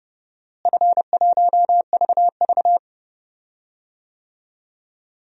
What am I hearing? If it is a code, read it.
Morse code, heard as a keyed tone, about 31 wpm, 700 Hz. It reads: F1VV
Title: DXCC-100 Call Signs - 4 Characters - Repeated 3x - 30wpm